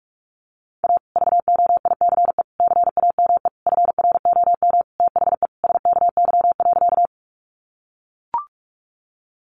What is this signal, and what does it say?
Morse code, heard as a keyed tone, about 44 wpm, 700 Hz. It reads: A VOICE CAME FROM THE SKY.
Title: Sentences from Top 1200 Words 45wpm